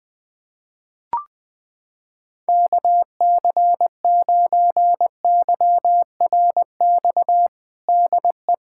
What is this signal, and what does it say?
Morse code, heard as a keyed tone, about 20 wpm, 700 Hz. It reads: KC9YRX DE
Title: USA Callsign DE Callsign 20wpm